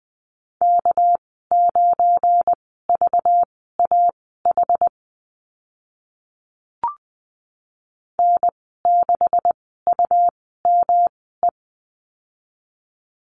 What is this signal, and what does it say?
Morse code, heard as a keyed tone, about 20 wpm, spaced as for 15 wpm, 700 Hz. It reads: K9VAH N6UME
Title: 213 Lesson 34g - Number 8 - Call Signs 20wpm